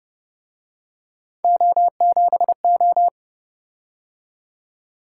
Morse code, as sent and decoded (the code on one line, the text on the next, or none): --- --... ---
O7O